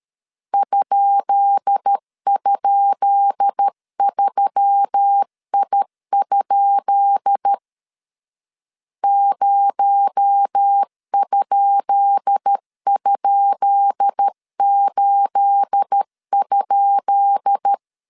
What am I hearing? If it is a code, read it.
Morse code, heard as a keyed tone, about 13 wpm, 800 Hz. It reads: ??3I? 0??8?
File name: LEZIONI / 10 SIGNS / 10B INT 12wpm 2414